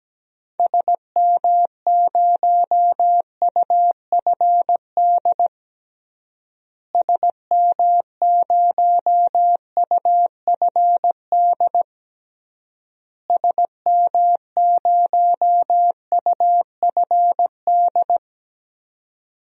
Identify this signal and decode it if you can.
Morse code, heard as a keyed tone, about 17 wpm, 700 Hz. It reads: SM0UFD SM0UFD SM0UFD
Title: DXCC-100 Call Signs - 6 Characters - Repeated 3x - 17wpm